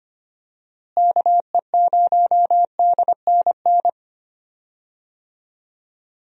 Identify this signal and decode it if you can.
Morse code, heard as a keyed tone, about 25 wpm, 700 Hz. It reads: KE0DNN
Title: USA Call Signs - 6 Characters - Repeated 2x - 25wpm